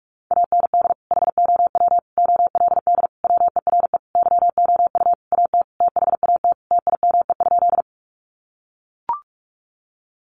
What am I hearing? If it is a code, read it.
Morse code, heard as a keyed tone, about 44 wpm, 700 Hz. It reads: AND HOW OLD WERE YOU AT THAT TIME?
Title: Sentences from Top 200 Words 45wpm